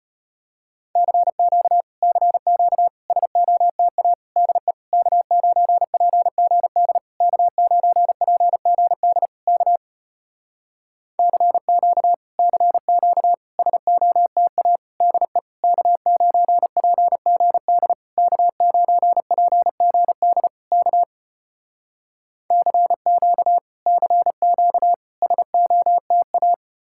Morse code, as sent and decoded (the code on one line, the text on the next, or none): -.-. --.- / -.-. --.- / ... --- - .- / -.. / -.- ----. .--. --. -.. / -.- ----. .--. --. -.. / -.- / -.-. --.- / -.-. --.- / ... --- - .- / -.. / -.- ----. .--. --. -.. / -.- ----. .--. --. -.. / -.- / -.-. --.- / -.-. --.- / ... --- - .-
CQ CQ SOTA DE K9PGD K9PGD K CQ CQ SOTA DE K9PGD K9PGD K CQ CQ SOTA